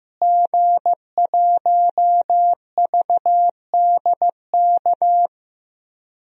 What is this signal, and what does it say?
Morse code, heard as a keyed tone, about 15 wpm, 700 Hz. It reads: G1VDK
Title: DXCC-100 Call Signs - 5 Characters - Repeated 3x - 15wpm